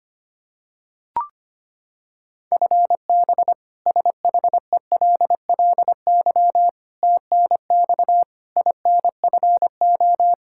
Morse code, heard as a keyed tone, about 25 wpm, 700 Hz. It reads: FB SHELLY TNX INFO
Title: K1USN SST QSO Practice 25wpm